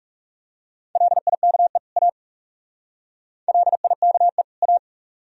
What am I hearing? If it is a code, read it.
Morse code, heard as a keyed tone, about 38 wpm, 700 Hz. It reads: LIKE A LIKE A
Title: Speed-Racing - Sets of 2 Words 30wpm